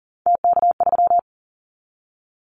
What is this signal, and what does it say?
Morse code, heard as a keyed tone, about 38 wpm, 700 Hz. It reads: TK3
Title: Callsign Prefixes for Top 100 DXCC - 40wpm